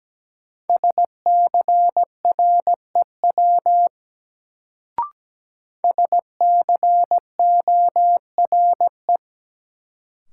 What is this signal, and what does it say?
Morse code, heard as a keyed tone, about 17 wpm, 700 Hz. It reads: SCREW SCORE
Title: Sentences from Top 1600 Words - New Word Drill Alt - 17wpm